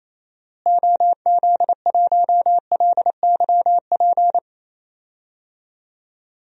Morse code, heard as a keyed tone, about 28 wpm, 700 Hz. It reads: OZ1LYP
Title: DXCC-100 Call Signs - 6 Characters - Repeated 3x - 28wpm